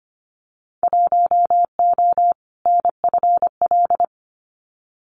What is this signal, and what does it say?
Morse code, heard as a keyed tone, about 25 wpm, 700 Hz. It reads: 1O NFL
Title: Winter Field Day Exchanges 25wpm